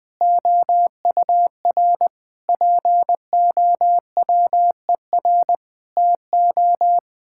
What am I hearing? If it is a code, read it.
Morse code, heard as a keyed tone, about 20 wpm, 700 Hz. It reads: OUR POWER TO
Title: Sets of 8 Words - 20wpm